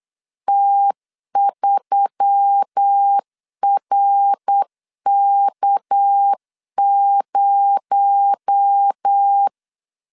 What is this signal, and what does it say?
Morse code, heard as a keyed tone, about 8 wpm, 800 Hz. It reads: T3RK0
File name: 8wpm